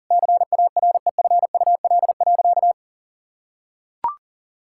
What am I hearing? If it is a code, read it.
Morse code, heard as a keyed tone, about 40 wpm, 700 Hz. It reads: CAREFUL.
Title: Sentences from Top 1600 Words 40wpm